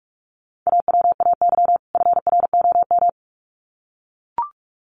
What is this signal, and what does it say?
Morse code, heard as a keyed tone, about 44 wpm, 700 Hz. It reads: AWAY FROM
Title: Sets of 2 Words 45wpm